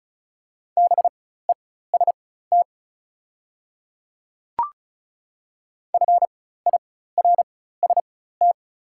Morse code, heard as a keyed tone, about 36 wpm, spaced as for 16 wpm, 690 Hz. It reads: BEST FIRST
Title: 4x Character Spacing - Top 500 Words - 35wpm